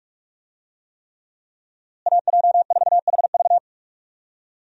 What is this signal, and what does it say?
Morse code, heard as a keyed tone, about 46 wpm, 700 Hz. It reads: AJ4HV
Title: USA Call Signs - 5 Characters - Repeated 3x - 45wpm